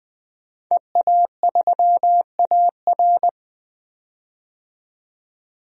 Morse code, heard as a keyed tone, about 21 wpm, 700 Hz. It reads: EA3AR
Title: DXCC-100 Call Signs - 5 Characters - 20wpm